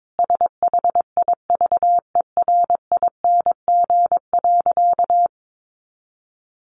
Morse code, heard as a keyed tone, about 22 wpm, 700 Hz. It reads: SHIVERING.
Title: A Sound Of Thunder 22wpm